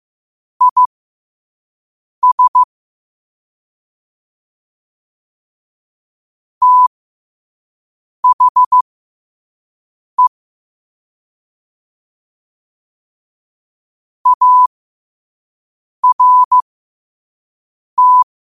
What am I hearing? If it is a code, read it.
Morse code, heard as a keyed tone, about 15 wpm, spaced as for 5 wpm, 1000 Hz. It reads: IS THE ART
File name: qotdfiles / 5wpm